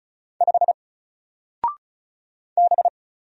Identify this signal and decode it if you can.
Morse code, heard as a keyed tone, about 34 wpm, 700 Hz. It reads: 5 B
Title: Single Letter-Number 35wpm